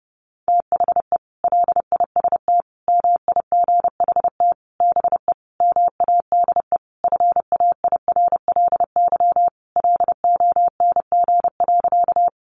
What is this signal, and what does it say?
Morse code, heard as a keyed tone, about 30 wpm, 700 Hz. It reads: THE LIST MIGHT BE MADE FAIRLY LONG.
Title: Sentences from Top 1100 Words 30wpm